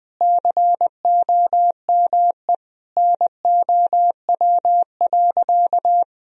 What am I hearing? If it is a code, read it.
Morse code, heard as a keyed tone, about 20 wpm, 700 Hz. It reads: COME NOW.
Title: Sentences from Top 200 Words and Abbreviations 20wpm